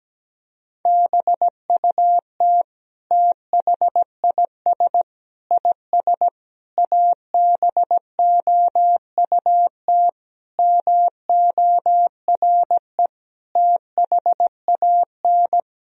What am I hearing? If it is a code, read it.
Morse code, heard as a keyed tone, about 17 wpm, 700 Hz. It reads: BUT THIS IS ABOUT MORE THAN
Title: Sentences from Top 1000 Words - Review Part 3 - 17wpm